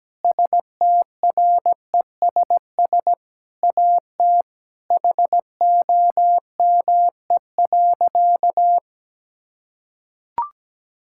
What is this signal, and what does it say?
Morse code, heard as a keyed tone, about 17 wpm, 700 Hz. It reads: STRESS AT HOME.
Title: Sentences from Top 1000 Words - Review Part 6 - 17wpm